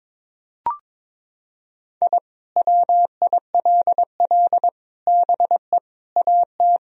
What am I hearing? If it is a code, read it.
Morse code, heard as a keyed tone, about 22 wpm, 700 Hz. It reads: I WILL BE AT
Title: Sentences from Top 1000 Words - Review Part 3 - 22wpm